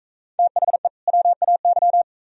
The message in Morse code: THE WAY

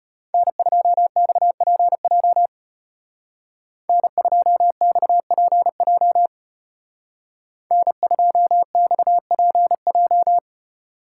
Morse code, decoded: N2XPJ N2XPJ N2XPJ

